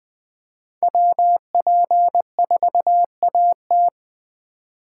Morse code, heard as a keyed tone, about 20 wpm, 700 Hz. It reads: WP4AT
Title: Speed Racing - USA 2x2 Call Signs 17wpm